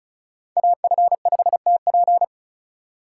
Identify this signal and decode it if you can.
Morse code, heard as a keyed tone, about 36 wpm, 700 Hz. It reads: AF5TP
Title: USA Call Signs - 5 Characters - 2x Character Spacing - 35wpm